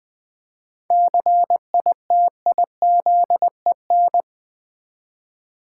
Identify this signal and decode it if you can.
Morse code, heard as a keyed tone, about 20 wpm, 700 Hz. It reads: CITIZEN
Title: 7 Letter Words - First 3 Letters Given - 20wpm